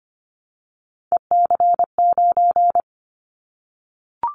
Morse code, heard as a keyed tone, about 25 wpm, 700 Hz. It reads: EC9
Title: Callsign Prefixes for Top 100 DXCC - 25wpm